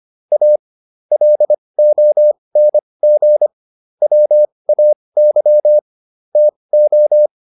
Morse code, heard as a keyed tone, about 25 wpm, spaced as for 20 wpm, 600 Hz. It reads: A LONG WAY TO